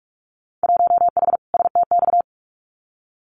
Morse code, heard as a keyed tone, about 44 wpm, 700 Hz. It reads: 1H STX
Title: Winter Field Day Exchanges Encore 45wpm